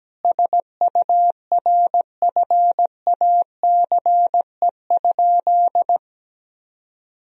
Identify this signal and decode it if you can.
Morse code, heard as a keyed tone, about 17 wpm, 700 Hz. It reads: SURFACE?
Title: Sentences from Top 1000 Words - Review Part 3 - 17wpm